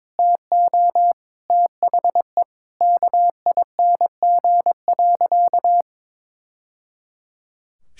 Morse code, heard as a keyed tone, about 22 wpm, 700 Hz. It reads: TO THE KING.
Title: Sentences from Top 1000 Words - Review Part 5 - 22wpm